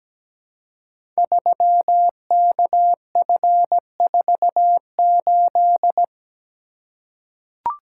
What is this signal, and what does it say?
Morse code, heard as a keyed tone, about 17 wpm, 700 Hz. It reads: 3KF48
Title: Code Groups - 5 Letter:Numbers - 17wpm